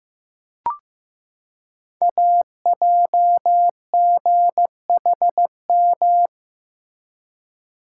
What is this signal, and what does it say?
Morse code, heard as a keyed tone, about 15 wpm, 700 Hz. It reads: AJGHM
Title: Code Groups - 5 Letters - 15wpm